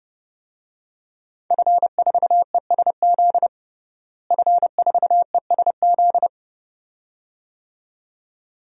Morse code, heard as a keyed tone, about 29 wpm, 700 Hz. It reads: F4ESZ F4ESZ
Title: DXCC-100 Call Signs - 5 Characters - Repeated 2x - 30wpm